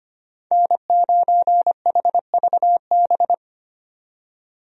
Morse code, heard as a keyed tone, about 25 wpm, 700 Hz. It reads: N9HVB
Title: USA Call Signs - 5 Characters - 25wpm